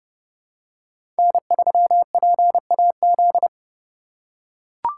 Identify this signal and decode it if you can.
Morse code, heard as a keyed tone, about 30 wpm, 700 Hz. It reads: N3PAZ
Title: USA Call Signs - 5 Characters - 4x Character Spacing - 30wpm